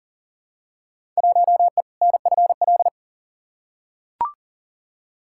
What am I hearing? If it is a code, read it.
Morse code, heard as a keyed tone, about 40 wpm, 700 Hz. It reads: 1E NFL
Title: Field Day Exchanges - 40wpm